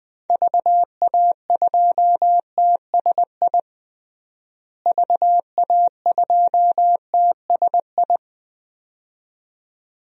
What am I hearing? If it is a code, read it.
Morse code, heard as a keyed tone, about 20 wpm, 700 Hz. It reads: VA2TSI VA2TSI
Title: DXCC-100 Call Signs - 6 Characters - Repeated 2x - 20wpm